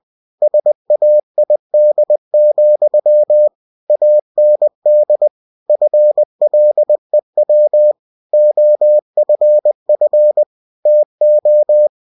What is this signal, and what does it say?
Morse code, heard as a keyed tone, about 20 wpm, 600 Hz. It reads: SAID, AND FLEW OFF TO
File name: cw-books / winnie-the-pooh / winnie-20wpm-page-0103